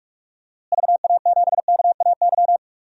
Morse code, heard as a keyed tone, about 46 wpm, 690 Hz. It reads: VA7KAY
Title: DXCC-100 Call Signs - 6 Characters - Repeated 2x - 45wpm